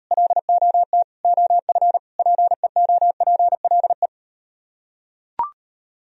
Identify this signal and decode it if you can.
Morse code, heard as a keyed tone, about 38 wpm, 700 Hz. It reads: LOT OF PEOPLE